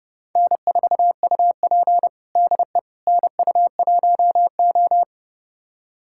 Morse code, heard as a keyed tone, about 30 wpm, 700 Hz. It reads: N4UP DE NU1O
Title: USA Callsign DE Callsign - 4 Characters - 30wpm